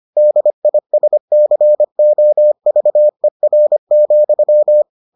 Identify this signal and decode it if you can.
Morse code, heard as a keyed tone, about 25 wpm, 600 Hz. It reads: DISCOVER,